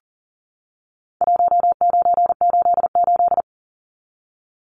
Morse code, heard as a keyed tone, about 40 wpm, 700 Hz. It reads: 1988
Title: Numbers - Contest Style - 4-Digit 40wpm